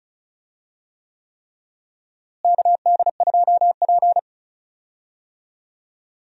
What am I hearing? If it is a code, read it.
Morse code, heard as a keyed tone, about 36 wpm, 700 Hz. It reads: KD2P
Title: USA Call Signs - 4 Characters - Repeated 3x - 35wpm